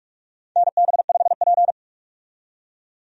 Morse code, heard as a keyed tone, about 46 wpm, 700 Hz. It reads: NB5P
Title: USA Call Signs - 4 Characters - 45wpm